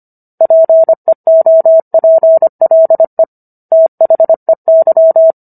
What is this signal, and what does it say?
Morse code, heard as a keyed tone, about 25 wpm, 650 Hz. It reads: PEOPLE THEY